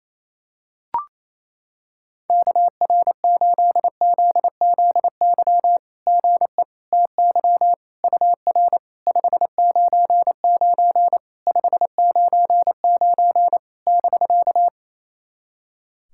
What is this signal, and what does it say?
Morse code, heard as a keyed tone, about 28 wpm, 690 Hz. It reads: KR8ZZY GE TY UR 599 599 <BK>
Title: SOTA QSO Practice 28wpm